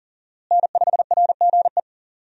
Morse code, heard as a keyed tone, about 40 wpm, 700 Hz. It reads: N5RGE
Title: USA Call Signs - 5 Characters - Repeated 2x - 40wpm